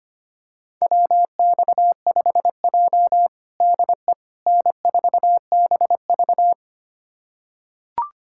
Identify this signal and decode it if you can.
Morse code, heard as a keyed tone, about 25 wpm, 700 Hz. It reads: WX5J DE N4BV
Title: USA Callsign DE Callsign - 4 Characters - 25wpm